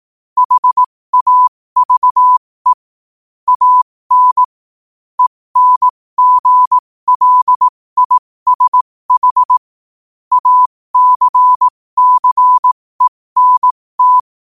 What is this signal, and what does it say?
Morse code, heard as a keyed tone, about 18 wpm, spaced as for 15 wpm, 1000 Hz. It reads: HAVE AN ENGLISH ACCENT
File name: qotdfiles / 15wpm